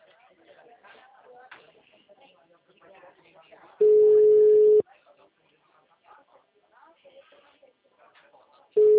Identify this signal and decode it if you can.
Ringback tone (Europe, 425 Hz), with speech.
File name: out- -2002- - - .2717